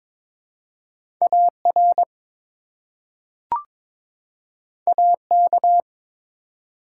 Morse code, heard as a keyed tone, about 21 wpm, 700 Hz. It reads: AR AK
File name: ARRL Sections v2 - 22wpm